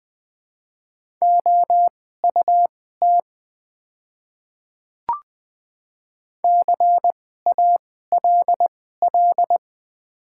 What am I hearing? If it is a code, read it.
Morse code, heard as a keyed tone, about 20 wpm, spaced as for 15 wpm, 700 Hz. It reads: OUT CALL
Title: 2x Character Spacing - Top 100 Words - 20wpm